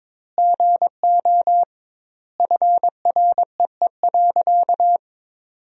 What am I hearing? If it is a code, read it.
Morse code, heard as a keyed tone, about 22 wpm, 700 Hz. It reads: GO FREE.
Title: 2x Word-Spacing - Sentences from Top 500 Words - 22wpm